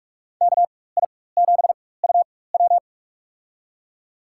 Morse code, heard as a keyed tone, about 44 wpm, spaced as for 21 wpm, 700 Hz. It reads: KI7UW
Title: USA Call Signs - 5 Characters - 4x Character Spacing - 45wpm